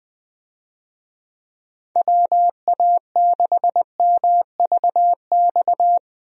Morse code, heard as a keyed tone, about 21 wpm, 700 Hz. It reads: WA6MVX
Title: USA Call Signs - 6 Characters - Repeated 2x - 20wpm